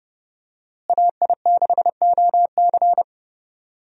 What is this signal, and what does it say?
Morse code, heard as a keyed tone, about 30 wpm, 700 Hz. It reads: AI6OC
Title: USA Call Signs - 5 Characters - Repeated 2x - 30wpm